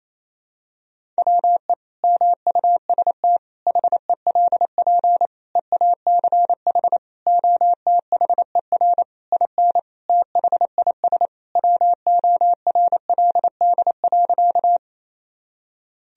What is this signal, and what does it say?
Morse code, heard as a keyed tone, about 28 wpm, 700 Hz. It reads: WE MUST HELP EACH OTHER IN THIS WORLD.
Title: Sentences from Top 1000 Words - Review Part 1 - 28wpm